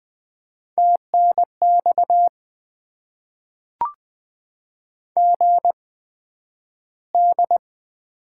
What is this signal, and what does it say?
Morse code, heard as a keyed tone, about 20 wpm, spaced as for 11 wpm, 700 Hz. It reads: TNX GD
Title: Top 90 QSO Elements - 8x Character Spacing - 20wpm